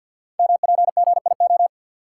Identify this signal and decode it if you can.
Morse code, heard as a keyed tone, about 50 wpm, 700 Hz. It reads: MPGIO
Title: Code Groups - 5 Letters - 50wpm